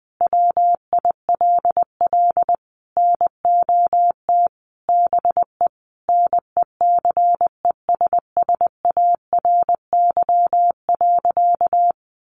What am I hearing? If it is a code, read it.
Morse code, heard as a keyed tone, about 20 wpm, 700 Hz. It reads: WILL NOT BE NECESSARY.
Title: Sentences from Top 800 Words 20wpm